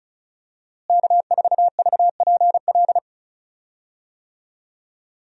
Morse code, heard as a keyed tone, about 34 wpm, 700 Hz. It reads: K4VPL
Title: USA Call Signs - 5 Characters - Repeated 3x - 35wpm